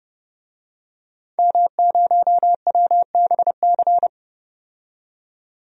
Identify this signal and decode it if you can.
Morse code, heard as a keyed tone, about 30 wpm, 700 Hz. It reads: M0WBC
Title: DXCC-100 Call Signs - 5 Characters - 30wpm